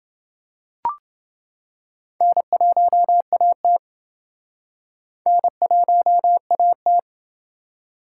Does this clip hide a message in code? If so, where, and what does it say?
Yes: Morse code from 2.2 to 7.0 s, heard as a keyed tone, about 29 wpm, 700 Hz. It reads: N1AT N1AT